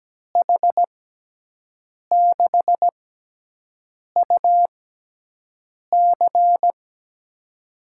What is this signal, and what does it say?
Morse code, heard as a keyed tone, about 17 wpm, spaced as for 6 wpm, 700 Hz. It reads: H6UC
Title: USA Call Signs - 6x Character Spacing - 17wpm